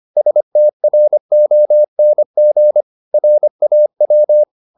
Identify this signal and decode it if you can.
Morse code, heard as a keyed tone, about 25 wpm, 600 Hz. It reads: STRONG RAW
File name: the old man and the sea part 1 25wpm